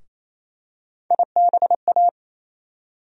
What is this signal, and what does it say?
Morse code, heard as a keyed tone, about 28 wpm, 700 Hz. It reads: IBA
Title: USA Call Sign Postfixes - 3 Letters 28wpm